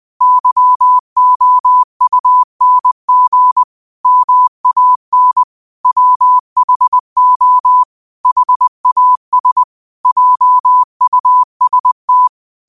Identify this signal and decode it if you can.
Morse code, heard as a keyed tone, about 20 wpm, 1000 Hz. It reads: YOUNG MAN WHO HAS JUST
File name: qotdfiles / 20wpm